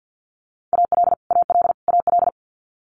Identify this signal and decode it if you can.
Morse code, heard as a keyed tone, about 50 wpm, 690 Hz. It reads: AL AL AL
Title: Multi-letter Phonemes Repeated 3x 50wpm